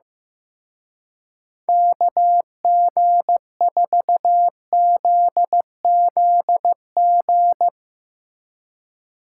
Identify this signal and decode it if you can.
Morse code, heard as a keyed tone, about 15 wpm, 700 Hz. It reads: KG4ZZG
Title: Speed Racing - USA 2x3 Call Signs 15wpm